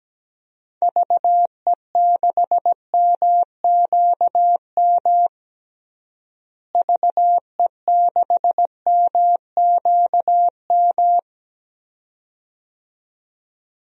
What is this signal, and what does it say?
Morse code, heard as a keyed tone, about 17 wpm, 700 Hz. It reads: VE6MQM VE6MQM
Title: DXCC-100 Call Signs - 6 Characters - Repeated 2x - 17wpm